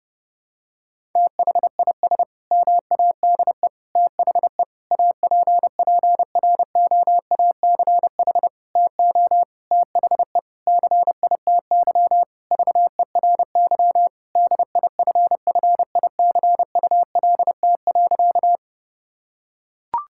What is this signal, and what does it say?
Morse code, heard as a keyed tone, about 30 wpm, 700 Hz. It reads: THIS MADE THE APPROACH TO THE CITY VERY DIFFICULT.
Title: Sentences from Top 1000 Words - Review Part 6 - 30wpm